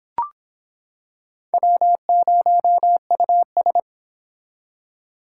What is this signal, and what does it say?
Morse code, heard as a keyed tone, about 25 wpm, 700 Hz. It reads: W0US